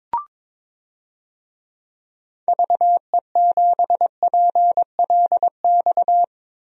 Morse code, heard as a keyed tone, about 23 wpm, 700 Hz. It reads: VE7PLX
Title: DXCC-100 Call Signs - 6 Characters - Repeated 2x - 22wpm